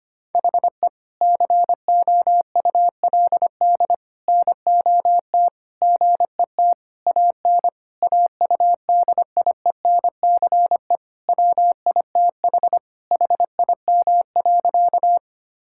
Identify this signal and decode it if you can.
Morse code, heard as a keyed tone, about 25 wpm, 700 Hz. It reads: HE COULD NOT GET AN AUDIENCE WITH HIM.
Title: Sentences from Top 1000 Words - Review Part 5 - 25wpm